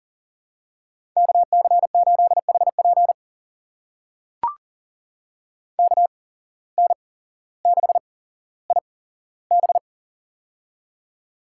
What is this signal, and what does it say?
Morse code, heard as a keyed tone, about 40 wpm, spaced as for 17 wpm, 700 Hz. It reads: KC8HP KN6IB
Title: USA Call Signs - 5 Characters - 8x Character Spacing - 40wpm